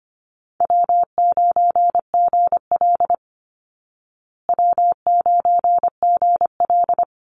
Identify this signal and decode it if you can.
Morse code, heard as a keyed tone, about 25 wpm, 700 Hz. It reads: W9GL W9GL
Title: Easy Call Signs Repeated 2x - 25wpm